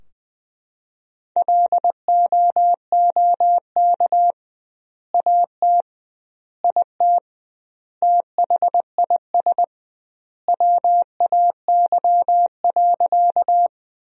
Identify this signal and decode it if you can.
Morse code, heard as a keyed tone, about 20 wpm, 700 Hz. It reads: LOOK AT IT THIS WAY.
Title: 2x Word-Spacing - Sentences from Top 100 Words - 20wpm